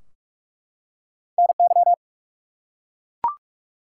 Morse code, heard as a keyed tone, about 44 wpm, 700 Hz. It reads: NY
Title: US States and Canadian Provinces 45wpm